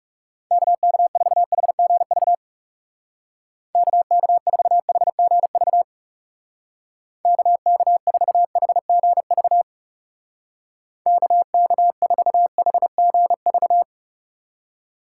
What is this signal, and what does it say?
Morse code, heard as a keyed tone, about 37 wpm, 700 Hz. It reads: KK4HGV KK4HGV KK4HGV KK4HGV
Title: Speed Racing - USA 2x3 Call Signs 30wpm